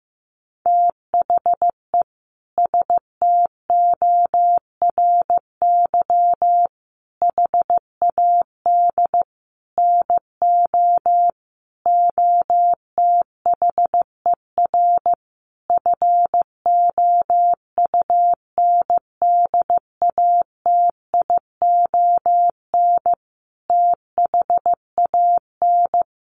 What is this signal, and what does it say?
Morse code, heard as a keyed tone, about 15 wpm, 700 Hz. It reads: THE STORY HAD NO OTHER FOUNDATION THAN
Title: Sentences from Top 1200 Words 15wpm